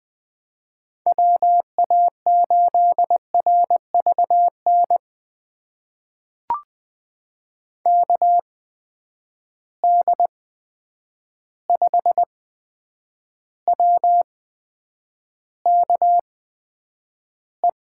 Morse code, heard as a keyed tone, about 20 wpm, spaced as for 9 wpm, 700 Hz. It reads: WA8RVN KD5WKE
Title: USA Call Signs - 6 Characters - 8x Character Spacing - 20wpm